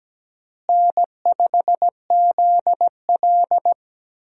Morse code, heard as a keyed tone, about 17 wpm, 700 Hz. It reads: N5ZL
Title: USA Call Signs - 4 Characters - Repeated 2x - 17wpm